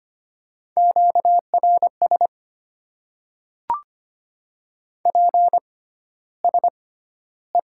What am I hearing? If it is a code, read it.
Morse code, heard as a keyed tone, about 25 wpm, spaced as for 13 wpm, 700 Hz. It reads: QRS PSE
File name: Top 90 QSO Elements - 6x Character Spacing - 25wpm